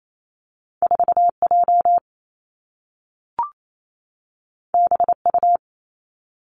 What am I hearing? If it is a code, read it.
Morse code, heard as a keyed tone, about 28 wpm, 700 Hz. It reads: 4 J B U
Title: Two Challenging Letter-Number Combinations 28wpm